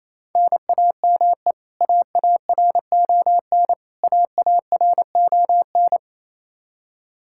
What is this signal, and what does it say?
Morse code, heard as a keyed tone, about 28 wpm, 700 Hz. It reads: NAME AARON AARON
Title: Practice QSO - Exchange 1 - Beginning - 28wpm